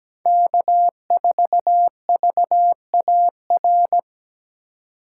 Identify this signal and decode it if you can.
Morse code, heard as a keyed tone, about 20 wpm, 700 Hz. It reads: K4VAR
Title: Speed Racing - USA 1x3 Call Signs 17wpm